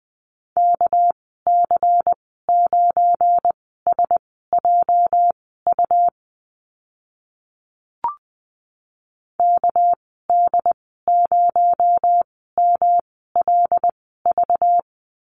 Morse code, heard as a keyed tone, about 20 wpm, spaced as for 15 wpm, 700 Hz. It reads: KC9SJU KD0MLV
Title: 238 Lesson 38g - Letter q - Call Signs 20wpm